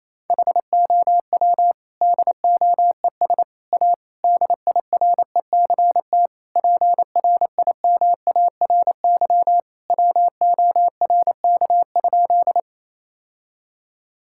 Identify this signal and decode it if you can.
Morse code, heard as a keyed tone, about 28 wpm, 700 Hz. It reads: HOW DOES A DIRECT PRIMARY WORK?
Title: Sentences from Top 1000 Words - Review Part 5 - 28wpm